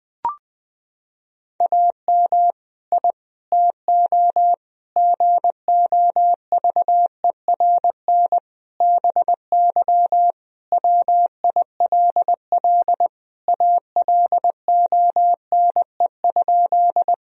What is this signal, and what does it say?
Morse code, heard as a keyed tone, about 21 wpm, 700 Hz. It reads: AM I TO GOVERN BY WILL ALONE?
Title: Sentences from Top 1000 Words - Review Part 2 - 20wpm